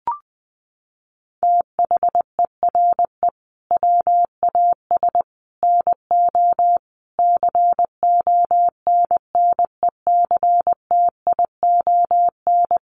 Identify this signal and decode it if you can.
Morse code, heard as a keyed tone, about 20 wpm, 700 Hz. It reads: THERE WAS NO CONNECTION
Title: Sentences from Top 1300 Words 20wpm